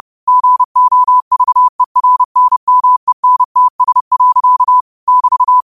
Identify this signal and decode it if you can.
Morse code, heard as a keyed tone, about 30 wpm, 1000 Hz. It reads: GOVERNMENTS. <BT>
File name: qotdfiles / 30wpm